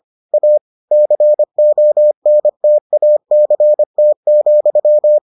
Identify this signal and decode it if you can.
Morse code, heard as a keyed tone, about 25 wpm, 600 Hz. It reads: A CONTACT,